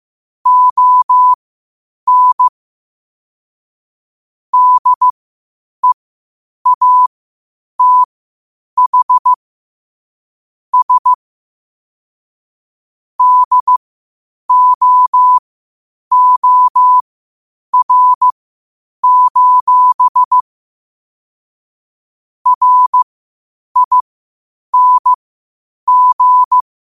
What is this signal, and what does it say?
Morse code, heard as a keyed tone, about 15 wpm, spaced as for 8 wpm, 1000 Hz. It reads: ON DEATH S DOOR: RING
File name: qotdfiles / 8wpm